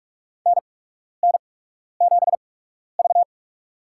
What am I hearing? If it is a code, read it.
Morse code, heard as a keyed tone, about 45 wpm, spaced as for 12 wpm, 690 Hz. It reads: NN7V
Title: USA Call Signs - 4 Characters - 8x Character Spacing - 45wpm